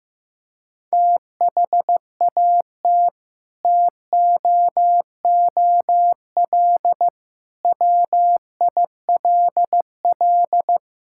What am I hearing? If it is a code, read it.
Morse code, heard as a keyed tone, about 15 wpm, 700 Hz. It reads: THAT TOOL WILL
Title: Sentences from Top 1000 Words - Review Part 6 - 15wpm